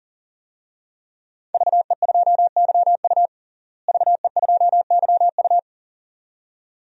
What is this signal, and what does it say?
Morse code, heard as a keyed tone, about 40 wpm, 700 Hz. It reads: VE2YU VE2YU
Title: DXCC-100 Call Signs - 5 Characters - Repeated 2x - 40wpm